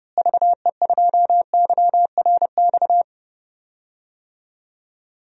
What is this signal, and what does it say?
Morse code, heard as a keyed tone, about 30 wpm, 700 Hz. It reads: VE2YRX
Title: DXCC-100 Call Signs - 6 Characters - Repeated 3x - 30wpm